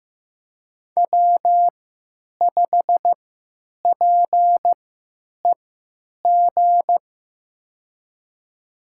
Morse code, heard as a keyed tone, about 15 wpm, spaced as for 8 wpm, 700 Hz. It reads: W5PEG
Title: USA Call Signs - 3x Character Spacing - 15wpm